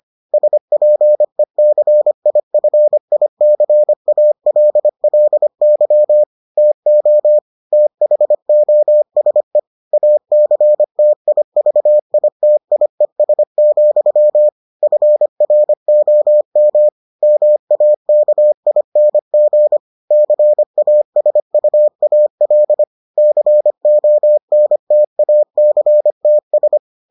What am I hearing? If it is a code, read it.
Morse code, heard as a keyed tone, about 25 wpm, 600 Hz. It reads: SPECIFICALLY TO THOSE ACTIVITIES, FROM MAKING CASUAL CONTACTS